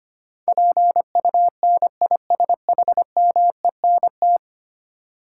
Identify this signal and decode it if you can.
Morse code, heard as a keyed tone, about 25 wpm, 700 Hz. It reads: PUNISHMENT
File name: Ten Letter Words - 25wpm